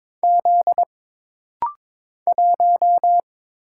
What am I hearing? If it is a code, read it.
Morse code, heard as a keyed tone, about 22 wpm, 700 Hz. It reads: Z 1
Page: This rapid-fire timing is geared to push your ICR (Instant Character Recognition) skills.